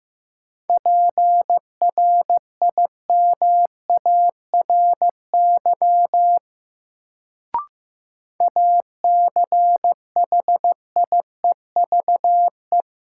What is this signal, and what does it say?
Morse code, heard as a keyed tone, about 15 wpm, 700 Hz. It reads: PRIMARY ACHIEVE
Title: Sentences from Top 1000 Words - Review Part 5 - New Word Drill - 15wpm